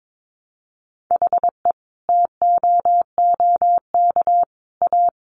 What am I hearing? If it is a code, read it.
Morse code, heard as a keyed tone, about 22 wpm, 700 Hz. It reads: HE TOOK A
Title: Sentences from Top 1000 Words 22wpm